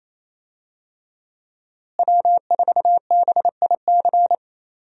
Morse code, heard as a keyed tone, about 28 wpm, 700 Hz. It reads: W4BIC
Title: USA Call Signs - 5 Characters - Repeated 3x - 28wpm